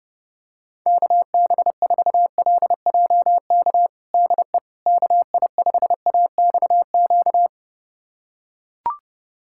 Morse code, heard as a keyed tone, about 32 wpm, 700 Hz. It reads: KB4LJK DE KI5AXQ
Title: USA Callsign DE Callsign - 6 Characters - 30wpm